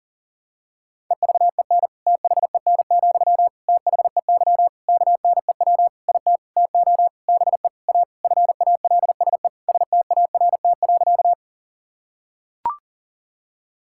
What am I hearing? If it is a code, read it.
Morse code, heard as a keyed tone, about 40 wpm, 700 Hz. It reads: EVEN THEN, THEY KNEW IT TO BE A FALSE START.
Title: Sentences from Top 1600 Words 40wpm